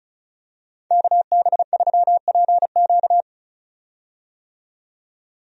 Morse code, heard as a keyed tone, about 34 wpm, 700 Hz. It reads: KB3PQ
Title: USA Call Signs - 5 Characters - 35wpm